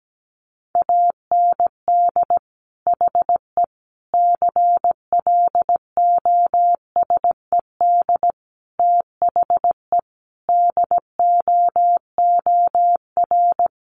Morse code, heard as a keyed tone, about 17 wpm, 700 Hz. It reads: AND HE CLOSED THE DOOR
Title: Sentences from Top 1500 Words 17wpm